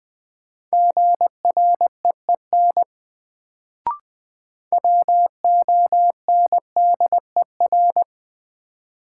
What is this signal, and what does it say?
Morse code, heard as a keyed tone, about 21 wpm, 700 Hz. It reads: GREEN WONDER
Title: Sentences from Top 500 Words - New Word Drill - 20wpm